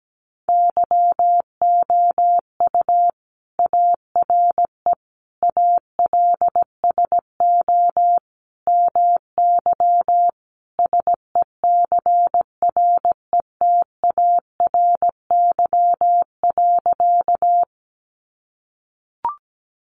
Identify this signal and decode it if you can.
Morse code, heard as a keyed tone, about 17 wpm, 700 Hz. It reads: YOU ARE ALSO MY SECRETARY.
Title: Sentences from Top 1300 Words 17wpm